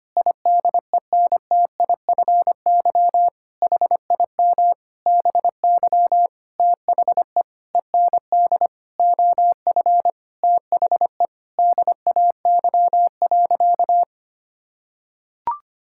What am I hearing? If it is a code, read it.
Morse code, heard as a keyed tone, about 25 wpm, 700 Hz. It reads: IDENTIFY HIM BY THE END OF THE DAY.
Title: Sentences from Top 1000 Words - Review Part 6 - 25wpm